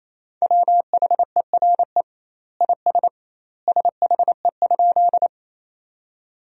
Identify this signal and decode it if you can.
Morse code, heard as a keyed tone, about 28 wpm, 700 Hz. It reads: WHERE IS SHE?
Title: 2x Word-Spacing - Sentences from Top 200 Words - 28wpm